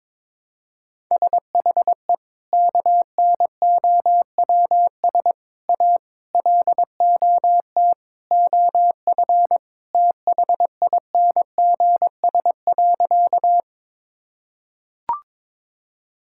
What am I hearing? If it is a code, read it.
Morse code, heard as a keyed tone, about 22 wpm, 700 Hz. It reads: SHE KNOWS A LOT OF THINGS.
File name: practice / 22wpm / Sentences with 4-Word Phrases - 22wpm